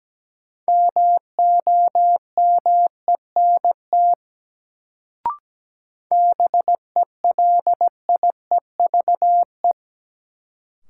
Morse code, heard as a keyed tone, about 17 wpm, 700 Hz. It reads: MOMENT BELIEVE
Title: Sentences from Top 1000 Words - New Word Drill - 17wpm